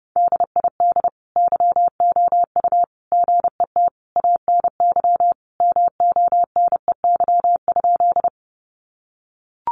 Morse code, heard as a keyed tone, about 30 wpm, 700 Hz. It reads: DID YOU GET ANY MONEY?
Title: Sentences from Top 400 Words 30wpm